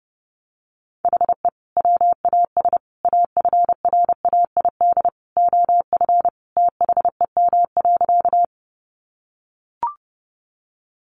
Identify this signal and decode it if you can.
Morse code, heard as a keyed tone, about 30 wpm, 700 Hz. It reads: HE WAS AFRAID OF THEM.
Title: Sentences from Top 1500 Words 30wpm